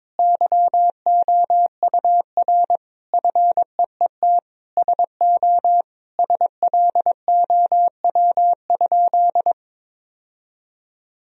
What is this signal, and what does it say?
Morse code, heard as a keyed tone, about 22 wpm, 700 Hz. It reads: YOUR FEET SO SLOW?
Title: Sentences from Top 1000 Words - Review Part 2 - 22wpm